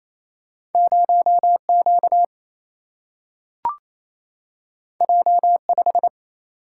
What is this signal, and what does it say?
Morse code, heard as a keyed tone, about 28 wpm, 700 Hz. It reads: 0Q J5
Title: Single-Letter Single-Number 28wpm